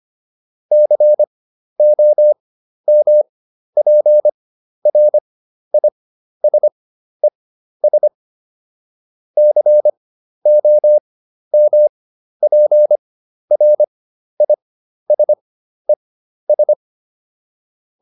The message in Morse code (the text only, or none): COMPRISES COMPRISES